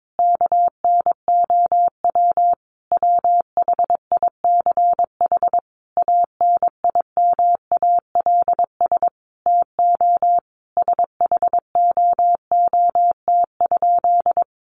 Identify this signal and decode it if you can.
Morse code, heard as a keyed tone, about 22 wpm, 700 Hz. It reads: KNOW WHICH ANIMALS TO SHOOT?
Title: A Sound Of Thunder 22wpm